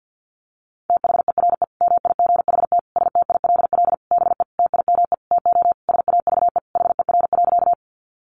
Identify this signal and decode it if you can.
Morse code, heard as a keyed tone, about 50 wpm, 700 Hz. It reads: THERE MIGHT STILL BE TIME TO SAVE HER.
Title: Sentences from Top 1100 Words 50wpm